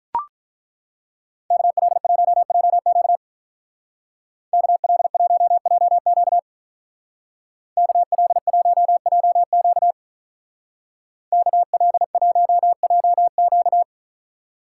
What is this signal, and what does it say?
Morse code, heard as a keyed tone, about 42 wpm, 700 Hz. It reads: KL1JQ KL1JQ KL1JQ KL1JQ
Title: Speed Racing - USA 2x2 Call Signs 35wpm